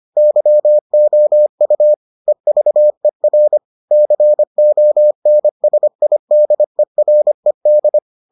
Morse code, heard as a keyed tone, about 25 wpm, 600 Hz. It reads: YOU EVER CONSIDERED